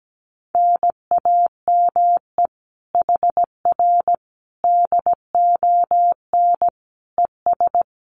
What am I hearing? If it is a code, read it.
Morse code, heard as a keyed tone, about 17 wpm, 700 Hz. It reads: NAME HR DON ES
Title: Common QSO Phrases Encore 17wpm